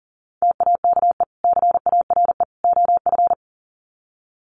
Morse code, heard as a keyed tone, about 40 wpm, 700 Hz. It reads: TAKE CARE OF
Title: Sets of 3 Words 40wpm